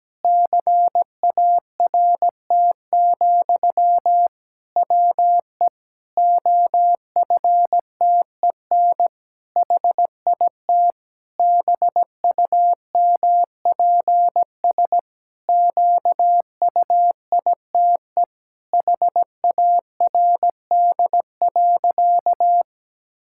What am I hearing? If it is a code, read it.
Morse code, heard as a keyed tone, about 17 wpm, 700 Hz. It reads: CART, WE OFTEN HIT BUMPS QUITE HARD.
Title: Pangram Challenge v2 - 17wpm